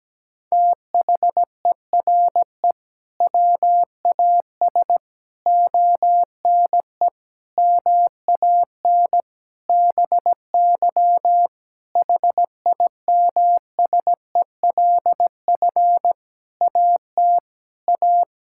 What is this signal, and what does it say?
Morse code, heard as a keyed tone, about 17 wpm, 700 Hz. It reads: THERE WAS ONE MAN BY HIMSELF AT A
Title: Sentences from Top 1600 Words 17wpm